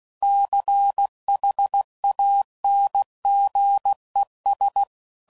Morse code, heard as a keyed tone, about 16 wpm, 800 Hz. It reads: CHANGES
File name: CW Code Groups @15WPM / Technical Subjects / Reactance